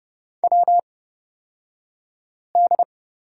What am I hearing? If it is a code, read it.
Morse code, heard as a keyed tone, about 29 wpm, spaced as for 5 wpm, 690 Hz. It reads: WD
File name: Lesson 36d - BT - Two Characters 30wpm